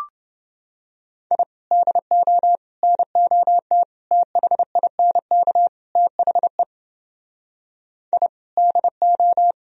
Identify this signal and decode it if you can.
Morse code, heard as a keyed tone, about 29 wpm, 700 Hz. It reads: I DO NOT THINK THE I DO